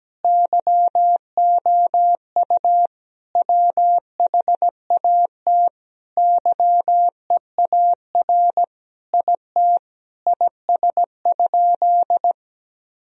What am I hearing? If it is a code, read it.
Morse code, heard as a keyed tone, about 17 wpm, 700 Hz. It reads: YOU WHAT YEAR IT IS?
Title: Sentences from Top 1000 Words - Review Part 1 - 17wpm